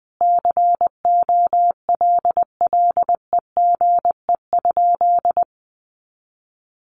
Morse code, heard as a keyed tone, about 20 wpm, 700 Hz. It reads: COLLEGE?
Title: Sentences from Top 1300 Words 20wpm